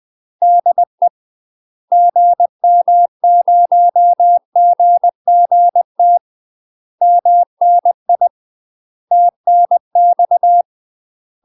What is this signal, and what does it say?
Morse code, heard as a keyed tone, about 20 wpm, 700 Hz. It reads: DE GM0GGT MNI TNX
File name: UK MASTER QSOs TRAINING / 20WPM / UK MASTER QSOs 0011